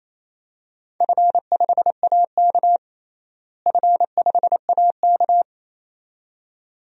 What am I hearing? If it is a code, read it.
Morse code, heard as a keyed tone, about 28 wpm, 700 Hz. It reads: F5AK F5AK
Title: DXCC-100 Call Signs - 4 Characters - Repeated 2x - 28wpm